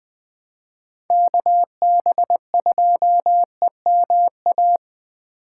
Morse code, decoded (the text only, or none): KB2EMA